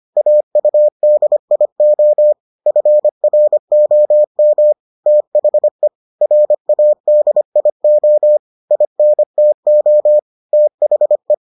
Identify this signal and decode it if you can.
Morse code, heard as a keyed tone, about 25 wpm, 600 Hz. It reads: AUDIO FROM THE RADIO INTO THE